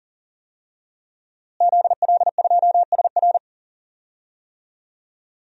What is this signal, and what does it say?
Morse code, heard as a keyed tone, about 39 wpm, 700 Hz. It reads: ZL2SR
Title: DXCC-100 Call Signs - 5 Characters - Repeated 3x - 40wpm